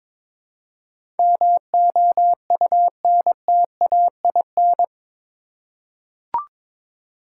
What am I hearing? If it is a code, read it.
Morse code, heard as a keyed tone, about 22 wpm, 700 Hz. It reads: MOUNTAIN
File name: practice / 22wpm / Eight Letter Words - 22wpm